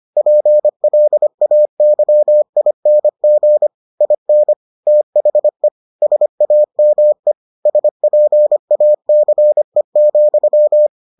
Morse code, heard as a keyed tone, about 25 wpm, 600 Hz. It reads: PLAYING IN THE SAME SPACE,